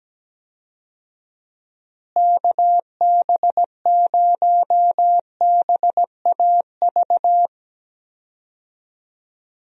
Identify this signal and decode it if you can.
Morse code, heard as a keyed tone, about 17 wpm, 700 Hz. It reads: KB0BAV
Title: USA Call Signs - 6 Characters - Repeated 2x - 17wpm